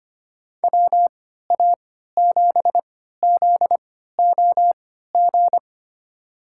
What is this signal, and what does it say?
Morse code, heard as a keyed tone, about 25 wpm, spaced as for 14 wpm, 700 Hz. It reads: WA7ZOG
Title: USA Call Signs - 3x Character Spacing - 25wpm